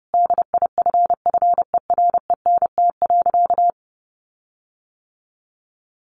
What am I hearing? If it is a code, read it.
Morse code, heard as a keyed tone, about 30 wpm, 700 Hz. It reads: DIFFERENT.
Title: Sentences from Top 1300 Words 30wpm